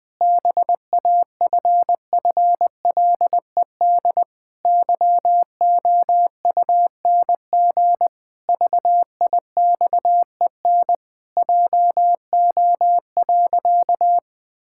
Morse code, heard as a keyed tone, about 20 wpm, 700 Hz. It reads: BAFFLED YOUNG VIXEN JO.
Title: Pangram Challenge v2 - 20wpm